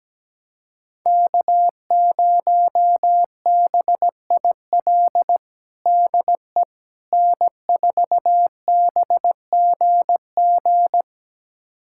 Morse code, heard as a keyed tone, about 17 wpm, 700 Hz. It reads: K0BIL DE N4BGG
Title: USA Callsign DE Callsign - 5 Characters - 17wpm